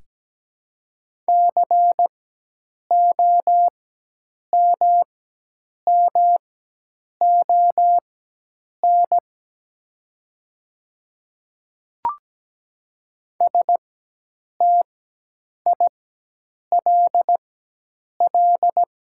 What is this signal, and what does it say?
Morse code, heard as a keyed tone, about 17 wpm, spaced as for 8 wpm, 700 Hz. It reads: COMMON STILL
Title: 4x Character Spacing - Top 500 Words - 17wpm